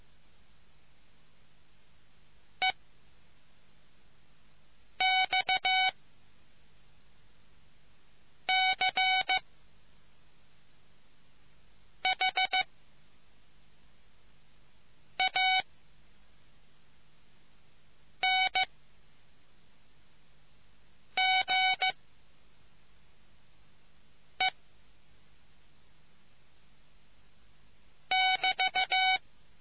Morse code, heard as a keyed tone, about 15 wpm, spaced as for 3 wpm, 750 Hz. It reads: EXCHANGE<BT>